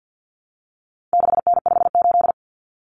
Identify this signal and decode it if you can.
Morse code, heard as a keyed tone, about 50 wpm, 700 Hz. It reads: 6N58
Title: Common Cut Numbers - 4-Digit 50wpm